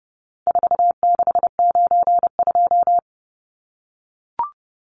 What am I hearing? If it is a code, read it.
Morse code, heard as a keyed tone, about 30 wpm, 700 Hz. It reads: 4692
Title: Numbers - 4-Digit 30wpm